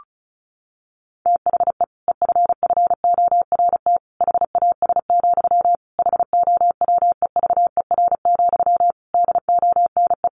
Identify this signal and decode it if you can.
Morse code, heard as a keyed tone, about 34 wpm, 700 Hz. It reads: THE EFFORT HAS, HOWEVER, DONE
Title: Sentences from Top 1200 Words 35wpm